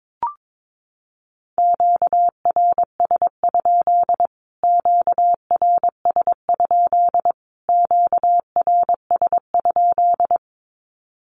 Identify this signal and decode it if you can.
Morse code, heard as a keyed tone, about 22 wpm, 700 Hz. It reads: QRS? QRS? QRS?
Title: Q-Codes Repeated 3x 22wpm